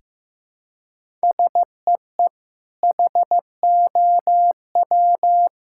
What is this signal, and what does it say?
Morse code, heard as a keyed tone, about 16 wpm, 700 Hz. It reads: SEE HOW
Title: Sentences from Top 1000 Words - Review Part 5 - 15wpm